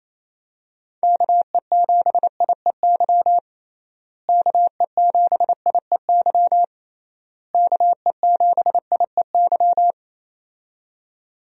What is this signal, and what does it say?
Morse code, heard as a keyed tone, about 28 wpm, 700 Hz. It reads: KE7IEY KE7IEY KE7IEY
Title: USA Call Signs - 6 Characters - Repeated 3x - 28wpm